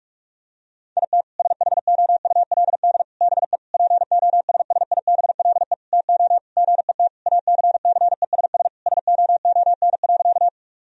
Morse code, heard as a keyed tone, about 46 wpm, 700 Hz. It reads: IT SHOULD BE POSSIBLE TO GET ACCESS SOON.
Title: Sentences from Top 1000 Words - Review Part 4 - 45wpm